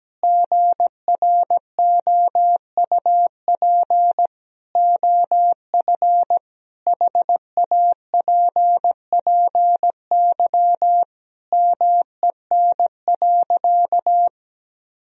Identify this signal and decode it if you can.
Morse code, heard as a keyed tone, about 17 wpm, 700 Hz. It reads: GROUP OF HAPPY MEN.
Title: Sentences from Top 1000 Words - Review Part 1 - 17wpm